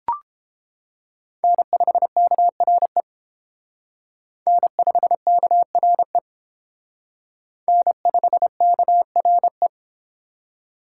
Morse code, heard as a keyed tone, about 29 wpm, 700 Hz. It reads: N5KRE N5KRE N5KRE